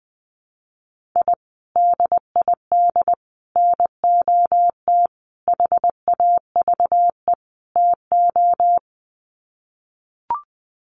Morse code, heard as a keyed tone, about 20 wpm, 700 Hz. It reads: I DID NOT HAVE TO
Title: Sets of 5 Words 20wpm